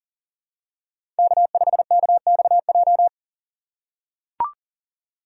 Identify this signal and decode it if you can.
Morse code, heard as a keyed tone, about 39 wpm, 700 Hz. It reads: K5KXJ
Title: K1USN SST QSO Practice - 40wpm